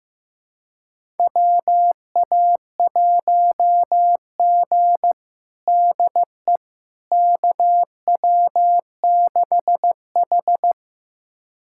Morse code, heard as a keyed tone, about 15 wpm, 700 Hz. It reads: WA1G DE KW6H
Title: USA Callsign DE Callsign - 4 Characters - 15wpm